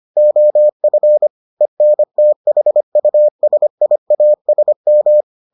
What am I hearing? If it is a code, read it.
Morse code, heard as a keyed tone, about 25 wpm, 600 Hz. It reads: OF ENTHUSIASM